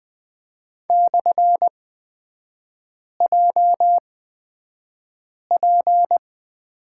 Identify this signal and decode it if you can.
Morse code, heard as a keyed tone, about 20 wpm, spaced as for 5 wpm, 700 Hz. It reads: /JP
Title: 236 Lesson 38e - Letter q - Three Characters 20wpm